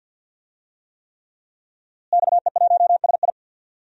Morse code, heard as a keyed tone, about 46 wpm, 700 Hz. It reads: XE1SI
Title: DXCC-100 Call Signs - 5 Characters - Repeated 3x - 50wpm